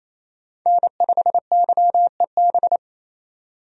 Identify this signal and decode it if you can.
Morse code, heard as a keyed tone, about 28 wpm, 700 Hz. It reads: N5YEB
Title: USA Call Signs - 5 Characters - 28wpm